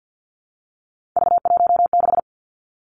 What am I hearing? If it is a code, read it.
Morse code, heard as a keyed tone, about 50 wpm, 700 Hz. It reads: V16
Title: Three Challenging Letter-Number Combinations 50wpm